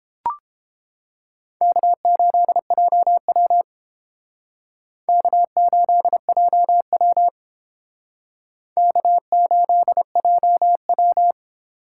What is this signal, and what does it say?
Morse code, heard as a keyed tone, about 30 wpm, 700 Hz. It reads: K8JW K8JW K8JW